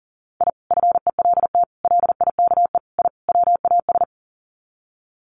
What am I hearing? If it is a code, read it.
Morse code, heard as a keyed tone, about 40 wpm, 700 Hz. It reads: I FELT LIKE I WAS
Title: Sets of 5 Words Encore 40wpm